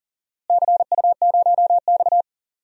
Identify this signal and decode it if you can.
Morse code, heard as a keyed tone, about 40 wpm, 700 Hz. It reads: CU0X
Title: DXCC-100 Call Signs - 4 Characters - Repeated 2x - 40wpm